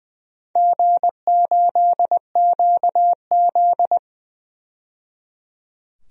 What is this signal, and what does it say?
Morse code, heard as a keyed tone, about 20 wpm, 700 Hz. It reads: G8QZ
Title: DXCC-100 Call Signs - 4 Characters - 20wpm